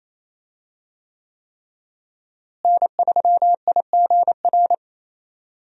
Morse code, heard as a keyed tone, about 28 wpm, 700 Hz. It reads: N3IGR
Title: USA Call Signs - 5 Characters - Repeated 3x - 28wpm